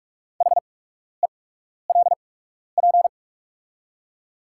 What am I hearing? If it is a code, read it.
Morse code, heard as a keyed tone, about 44 wpm, spaced as for 12 wpm, 690 Hz. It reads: HELP
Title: 8x Character Spacing - Top 200 Words - 45wpm